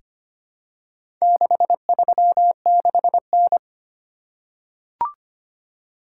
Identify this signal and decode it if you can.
Morse code, heard as a keyed tone, about 25 wpm, 700 Hz. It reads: 636N
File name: practice / 25wpm / Four Letter-Number Combinations - 25wpm